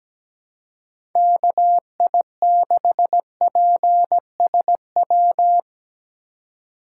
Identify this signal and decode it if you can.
Morse code, heard as a keyed tone, about 17 wpm, 700 Hz. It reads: KI6PSW
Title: USA Call Signs - 6 Characters - 4x Character Spacing - 17wpm